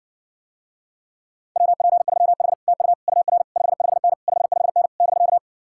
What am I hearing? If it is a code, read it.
Morse code, heard as a keyed tone, about 60 wpm, 700 Hz. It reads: WP3H TU UR 55N 55N <BK>